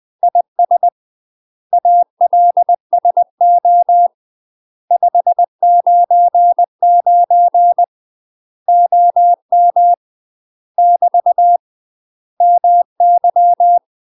Morse code, heard as a keyed tone, about 20 wpm, 700 Hz. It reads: IS ALSO 599 OM <BT> MY
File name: UK MASTER QSOs TRAINING / 20WPM / UK MASTER QSOs 0044